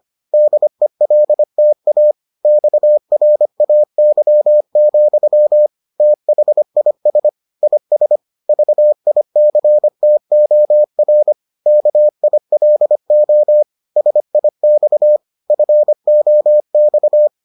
Morse code, heard as a keyed tone, about 25 wpm, 600 Hz. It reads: DELTA XRAY, THIS IS VICTOR KILO SIX FOX